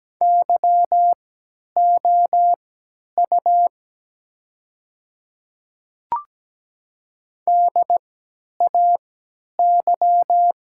Morse code, heard as a keyed tone, about 17 wpm, spaced as for 10 wpm, 700 Hz. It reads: YOU DAY
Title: 3x Character Spacing - Top 100 Words - 17wpm